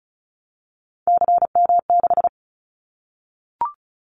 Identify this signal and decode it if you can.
Morse code, heard as a keyed tone, about 36 wpm, 700 Hz. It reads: CM6
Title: Callsign Prefixes for Top 100 DXCC - 35wpm